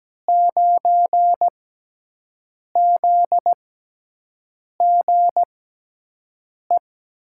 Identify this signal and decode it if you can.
Morse code, heard as a keyed tone, about 17 wpm, spaced as for 6 wpm, 700 Hz. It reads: 9ZGE
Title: USA Call Signs - 5 Characters - 6x Character Spacing - 17wpm